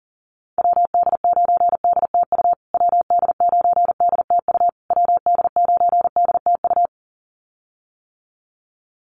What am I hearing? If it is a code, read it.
Morse code, heard as a keyed tone, about 40 wpm, 700 Hz. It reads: WD9DTU WD9DTU WD9DTU
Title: USA Call Signs Repeated 3x 40wpm